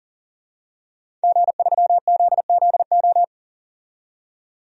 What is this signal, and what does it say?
Morse code, heard as a keyed tone, about 40 wpm, 700 Hz. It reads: G3ZZO
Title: DXCC-100 Call Signs - 5 Characters - 40wpm